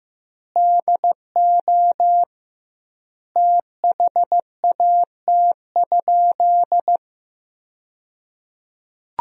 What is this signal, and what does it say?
Morse code, heard as a keyed tone, about 15 wpm, 700 Hz. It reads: DO THAT?
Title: 2x Word-Spacing - Sentences from Top 100 Words - 15wpm